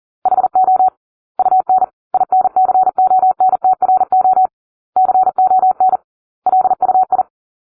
Morse code, heard as a keyed tone, about 40 wpm, 750 Hz. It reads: HO UN INCONTRO CON LUI